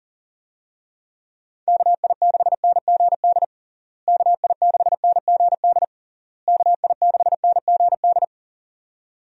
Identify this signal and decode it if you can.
Morse code, heard as a keyed tone, about 40 wpm, 700 Hz. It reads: KI6NGD KI6NGD KI6NGD
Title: USA Call Signs - 6 Characters - Repeated 3x - 40wpm